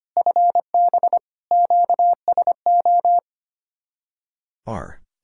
Morse code, heard as a keyed tone, about 25 wpm, 700 Hz. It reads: FB QSO
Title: Speed-Racing Common QSO Phrases Encore - 25wpm